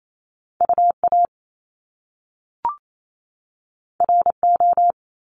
Morse code, heard as a keyed tone, about 28 wpm, 700 Hz. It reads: UA RO
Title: Top Two Letter Combinations in Words 28wpm